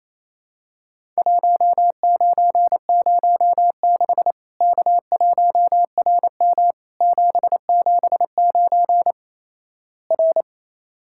Morse code, heard as a keyed tone, about 28 wpm, 700 Hz. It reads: 1906 K1RM 779 R
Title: Worked All Europe DX Contest - QTC Exchange - 28wpm